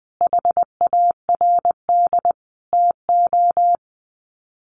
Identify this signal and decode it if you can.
Morse code, heard as a keyed tone, about 20 wpm, 700 Hz. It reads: HARD TO
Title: Sets of 5 Words Encore2 20wpm